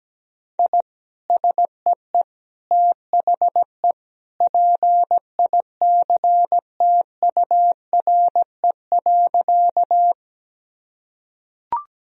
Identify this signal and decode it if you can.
Morse code, heard as a keyed tone, about 17 wpm, 700 Hz. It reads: I SEE THE PICTURE.
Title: Sentences from Top 1000 Words - Review Part 1 - 17wpm